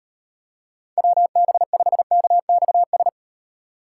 Morse code, heard as a keyed tone, about 35 wpm, 700 Hz. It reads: WB5KXS